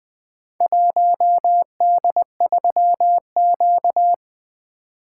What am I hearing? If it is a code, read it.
Morse code, heard as a keyed tone, about 20 wpm, 700 Hz. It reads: 1D3Q
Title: Four Letter-Number Combinations - 20wpm